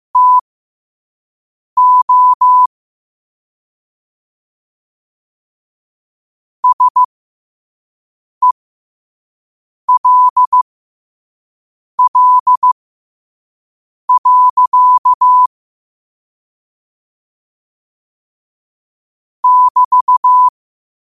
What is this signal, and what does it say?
Morse code, heard as a keyed tone, about 15 wpm, spaced as for 5 wpm, 1000 Hz. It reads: TO SELL. <BT>
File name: 2026-02-15-5wpm